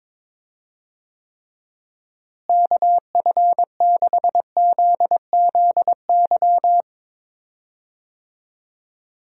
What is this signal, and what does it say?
Morse code, heard as a keyed tone, about 21 wpm, 700 Hz. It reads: KF6ZZY
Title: USA Call Signs - 6 Characters - Repeated 3x - 22wpm